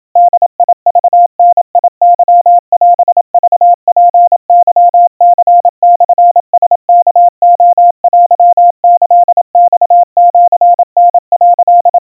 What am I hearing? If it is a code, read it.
Morse code (Japanese Wabun code), heard as a keyed tone, about 27 wpm, 700 Hz. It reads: ボクダケオクツケニモラワレテキマシタ」
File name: kawa / Morse sample / kamyu 100 cpm 700 Hz Crispy